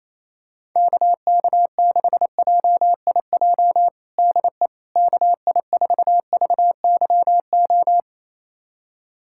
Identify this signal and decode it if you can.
Morse code, heard as a keyed tone, about 28 wpm, 700 Hz. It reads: KK6JIJ DE KI4VYO
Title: USA Callsign DE Callsign - 6 Characters - 28wpm